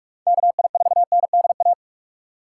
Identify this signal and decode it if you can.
Morse code, heard as a keyed tone, about 46 wpm, 700 Hz. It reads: KI4NDA
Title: USA Call Signs - 3x Character Spacing - 45wpm